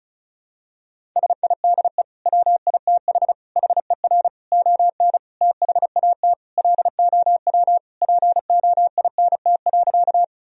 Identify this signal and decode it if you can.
Morse code, heard as a keyed tone, about 35 wpm, 700 Hz. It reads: SIDE WITH HER ON THAT LOW POINT.
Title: Sentences from Top 1000 Words - Review Part 1 - 35wpm